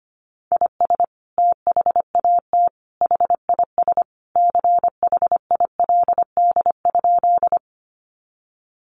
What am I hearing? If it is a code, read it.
Morse code, heard as a keyed tone, about 25 wpm, 700 Hz. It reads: IS THAT HIS CHILD?
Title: Sentences from Top 800 Words 25wpm